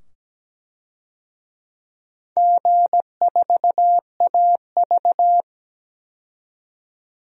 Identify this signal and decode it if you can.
Morse code, heard as a keyed tone, about 17 wpm, 700 Hz. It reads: G4AV
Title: DXCC-100 Call Signs - 4 Characters - 17wpm